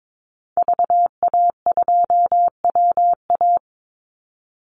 Morse code, heard as a keyed tone, about 22 wpm, 700 Hz. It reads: VA2WA
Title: Call Signs Repeated 2x - 22wpm